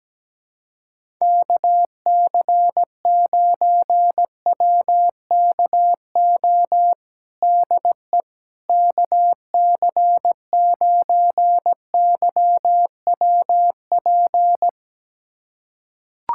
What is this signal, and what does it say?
Morse code, heard as a keyed tone, about 17 wpm, 700 Hz. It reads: KC9WKO DE KC9YWP
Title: USA Callsign DE Callsign 17wpm